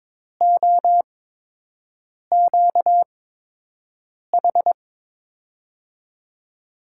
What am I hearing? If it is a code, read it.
Morse code, heard as a keyed tone, about 22 wpm, 700 Hz. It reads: O Q H
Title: USA Call Signs - 6 Characters - 8x Character Spacing - 22wpm